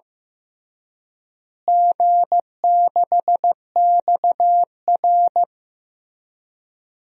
Morse code, heard as a keyed tone, about 15 wpm, 700 Hz. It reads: G6XR
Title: DXCC-100 Call Signs - 4 Characters - Repeated 3x - 15wpm